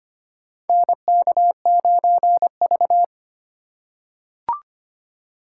Morse code, heard as a keyed tone, about 25 wpm, 700 Hz. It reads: NK9V
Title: USA Call Signs - 4 Characters - 2x Character Spacing - 25wpm